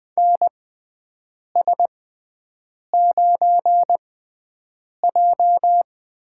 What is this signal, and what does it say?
Morse code, heard as a keyed tone, about 20 wpm, spaced as for 7 wpm, 700 Hz. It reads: NS9J
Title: USA Call Signs - 6x Character Spacing - 20wpm